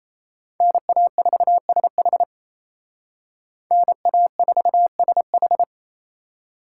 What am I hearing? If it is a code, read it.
Morse code, heard as a keyed tone, about 33 wpm, 700 Hz. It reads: NA4SH NA4SH
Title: Speed Racing - USA 2x2 Call Signs 28wpm